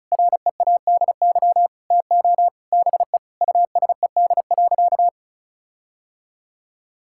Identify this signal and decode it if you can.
Morse code, heard as a keyed tone, about 36 wpm, 700 Hz. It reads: READY TO BE USED.
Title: Sentences from Top 1600 Words 35wpm